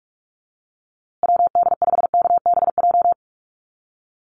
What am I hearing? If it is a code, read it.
Morse code, heard as a keyed tone, about 44 wpm, 700 Hz. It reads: WD5KBJ
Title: USA Call Signs Repeated 2x 45wpm